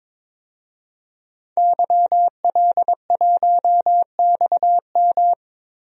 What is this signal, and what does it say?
Morse code, heard as a keyed tone, about 21 wpm, 700 Hz. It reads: YL1XM
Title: DXCC-100 Call Signs - 5 Characters - Repeated 2x - 22wpm